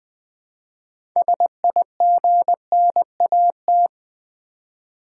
Morse code, heard as a keyed tone, about 20 wpm, 700 Hz. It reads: SIGNAT
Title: Build A Word - 3 Letters At A Time - 9-Letter Words - 20wpm